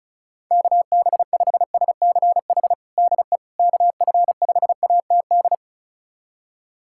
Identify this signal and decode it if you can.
Morse code, heard as a keyed tone, about 36 wpm, 700 Hz. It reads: KB5SCH DE KF5ATD
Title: USA Callsign DE Callsign - 6 Characters - 35wpm